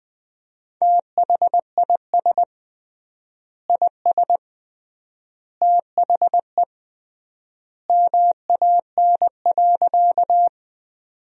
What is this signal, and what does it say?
Morse code, heard as a keyed tone, about 20 wpm, 700 Hz. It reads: THIS IS THE MAN.
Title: 3x Word-Spacing - Sentences from Top 200 Words - 20wpm